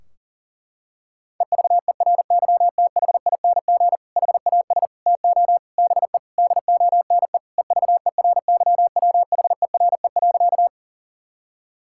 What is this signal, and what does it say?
Morse code, heard as a keyed tone, about 40 wpm, 700 Hz. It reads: EVERYTHING HAS TO BE DONE EVERYWHERE.
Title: Sentences from Top 1600 Words 40wpm